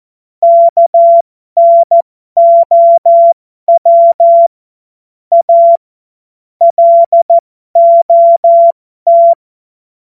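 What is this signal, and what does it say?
Morse code, heard as a keyed tone, about 14 wpm, spaced as for 12 wpm, 680 Hz. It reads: KNOW A LOT
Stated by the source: Morse code joke.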